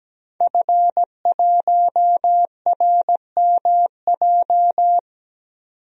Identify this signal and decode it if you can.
Morse code, heard as a keyed tone, about 17 wpm, 700 Hz. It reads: F1RMJ
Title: DXCC-100 Call Signs - 5 Characters - 17wpm